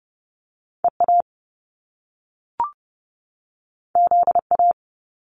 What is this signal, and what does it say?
Morse code, heard as a keyed tone, about 30 wpm, 700 Hz. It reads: EA ZA
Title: Top Two Letter Combinations in Words 30wpm